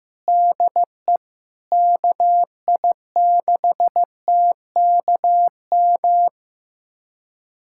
Morse code, heard as a keyed tone, about 15 wpm, 700 Hz. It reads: DE KI6TKM
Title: USA Callsign DE Callsign - 6 Characters - 15wpm